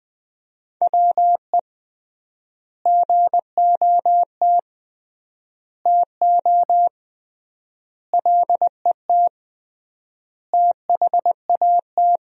Morse code, heard as a keyed tone, about 21 wpm, 700 Hz. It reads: WE GOT TO LET THAT
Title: 3x Word-Spacing - Sentences from Top 300 Words - 20wpm